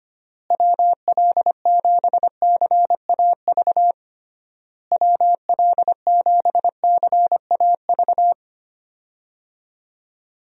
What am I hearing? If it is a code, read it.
Morse code, heard as a keyed tone, about 25 wpm, 700 Hz. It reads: WL7CAV WL7CAV
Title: USA Call Signs - 6 Characters - Repeated 3x - 25wpm